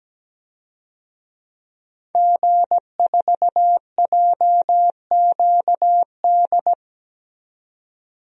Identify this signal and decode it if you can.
Morse code, heard as a keyed tone, about 17 wpm, 700 Hz. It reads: G4JQD
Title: DXCC-100 Call Signs - 5 Characters - 17wpm